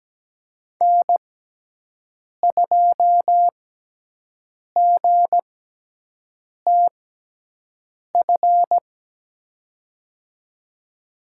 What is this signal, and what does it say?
Morse code, heard as a keyed tone, about 17 wpm, spaced as for 6 wpm, 700 Hz. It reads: N2GTF
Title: USA Call Signs - 6x Character Spacing - 17wpm